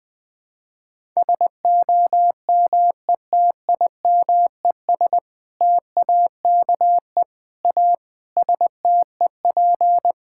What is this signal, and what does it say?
Morse code, heard as a keyed tone, about 20 wpm, 700 Hz. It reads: SOMETIMES TAKE A STEP
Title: Sentences with 6-Word Phrases - 20wpm